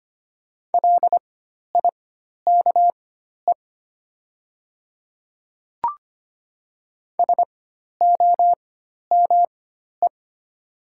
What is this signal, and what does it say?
Morse code, heard as a keyed tone, about 25 wpm, spaced as for 12 wpm, 700 Hz. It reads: LIKE SOME
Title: 4x Character Spacing - Top 100 Words - 25wpm